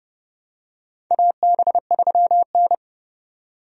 Morse code, heard as a keyed tone, about 30 wpm, 700 Hz. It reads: AB3N
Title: USA Call Signs - 4 Characters - 30wpm